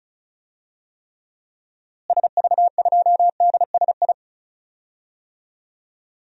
Morse code, heard as a keyed tone, about 36 wpm, 700 Hz. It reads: SV2DSI
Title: DXCC-100 Call Signs - 6 Characters - Repeated 3x - 35wpm